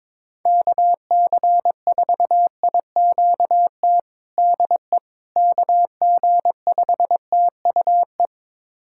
Morse code, heard as a keyed tone, about 22 wpm, 700 Hz. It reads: KC4IQT DE KG5TUE
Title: USA Callsign DE Callsign 22wpm